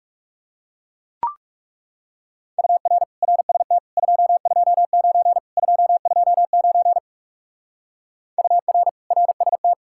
Morse code, heard as a keyed tone, about 37 wpm, 700 Hz. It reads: UR RST 229 229 UR RST
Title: Speed-Racing Common QSO Phrases Encore - 30wpm